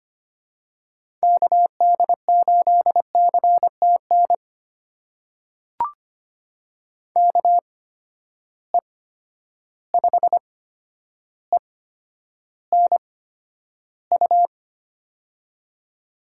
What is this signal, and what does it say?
Morse code, heard as a keyed tone, about 25 wpm, spaced as for 11 wpm, 700 Hz. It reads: KD8CTN KE5ENU
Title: USA Call Signs - 6 Characters - 8x Character Spacing - 25wpm